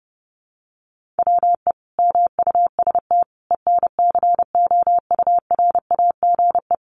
Morse code, heard as a keyed tone, about 30 wpm, 700 Hz. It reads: WE MUST ENCOURAGE
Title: Sentences from Top 1400 Words 30wpm